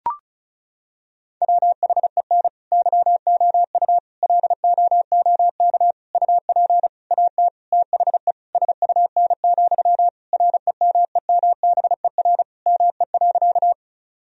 Morse code, heard as a keyed tone, about 34 wpm, 700 Hz. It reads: WHEN YOU LOOK UP AT THE SUN, REMEMBER ME.
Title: Sentences from Top 1000 Words - Review Part 1 - 35wpm